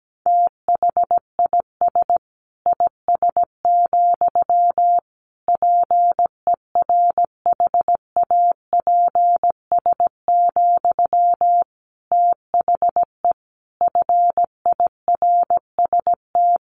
Morse code, heard as a keyed tone, about 17 wpm, 700 Hz. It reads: THIS IS, PERHAPS, THE FIRST
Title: Sentences from Top 1500 Words 17wpm